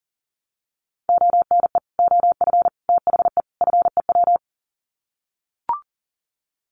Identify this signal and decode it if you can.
Morse code, heard as a keyed tone, about 40 wpm, 700 Hz. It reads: ONE OF THE FEW
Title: Sets of 4 Words 40wpm